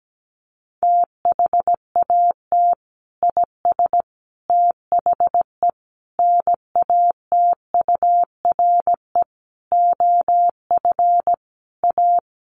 Morse code, heard as a keyed tone, about 17 wpm, 700 Hz. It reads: THAT IS THE NATURE OF A
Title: Sentences from Top 1400 Words 17wpm